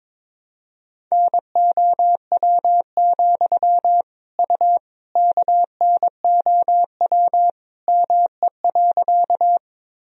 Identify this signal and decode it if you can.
Morse code, heard as a keyed tone, about 22 wpm, 700 Hz. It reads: NOW, U KNOW ME.
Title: Sentences from Top 200 Words and Abbreviations 22wpm